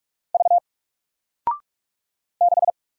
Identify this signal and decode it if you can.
Morse code, heard as a keyed tone, about 44 wpm, 700 Hz. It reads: V 6
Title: Single Character (Letter:Number:Punctuation:Prosign) 45wpm